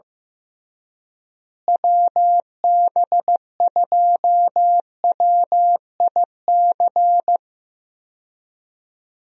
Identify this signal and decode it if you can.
Morse code, heard as a keyed tone, about 16 wpm, 700 Hz. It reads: WB2WIC
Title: USA Call Signs - 6 Characters - Repeated 3x - 15wpm